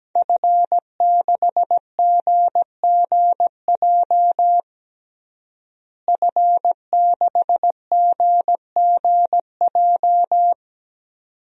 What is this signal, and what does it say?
Morse code, heard as a keyed tone, about 17 wpm, 700 Hz. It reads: F6GGJ F6GGJ
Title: DXCC-100 Call Signs - 5 Characters - Repeated 2x - 17wpm